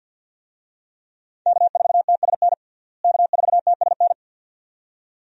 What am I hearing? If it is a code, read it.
Morse code, heard as a keyed tone, about 47 wpm, 700 Hz. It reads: K4TSN K4TSN
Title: USA Call Signs - 5 Characters - Repeated 2x - 50wpm